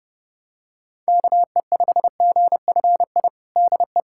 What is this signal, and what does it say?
Morse code, heard as a keyed tone, about 30 wpm, 700 Hz. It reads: KE5GFI DE
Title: USA Callsign DE Callsign 30wpm